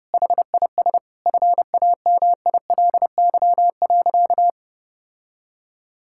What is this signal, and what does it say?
Morse code, heard as a keyed tone, about 32 wpm, 700 Hz. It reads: HIS FAMILY.
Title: Sentences from Top 1000 Words - Review Part 6 - 30wpm